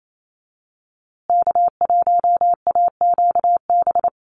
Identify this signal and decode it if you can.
Morse code, heard as a keyed tone, about 28 wpm, 700 Hz. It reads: K1AQB
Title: USA Call Signs Repeated 3x 28wpm